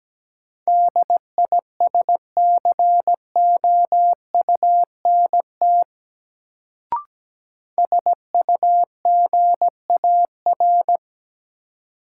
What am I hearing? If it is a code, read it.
Morse code, heard as a keyed tone, about 17 wpm, 700 Hz. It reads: DISCOUNT SUGAR
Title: Sentences from Top 1200 Words - New Word Drill Alt - 17wpm